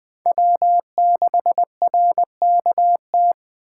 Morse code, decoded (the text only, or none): W6RKT